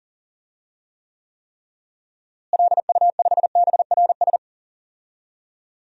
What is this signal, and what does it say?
Morse code, heard as a keyed tone, about 40 wpm, 700 Hz. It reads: LU5BRS
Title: DXCC-100 Call Signs - 6 Characters - Repeated 3x - 40wpm